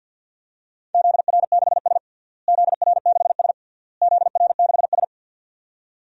Morse code, heard as a keyed tone, about 47 wpm, 700 Hz. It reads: ZR6S ZR6S ZR6S
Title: DXCC-100 Call Signs - 4 Characters - Repeated 3x - 50wpm